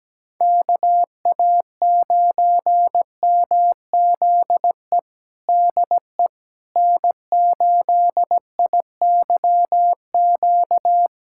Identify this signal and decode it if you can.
Morse code, heard as a keyed tone, about 17 wpm, 700 Hz. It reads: KA9MZE DE N8IYQ
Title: USA Callsign DE Callsign 17wpm